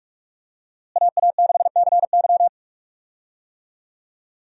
Morse code, heard as a keyed tone, about 46 wpm, 700 Hz. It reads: AA6CY
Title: USA Call Signs - 5 Characters - Repeated 3x - 45wpm